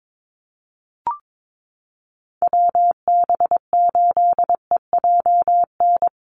Morse code, heard as a keyed tone, about 22 wpm, 700 Hz. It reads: WB8EJN
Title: Call Signs - 22wpm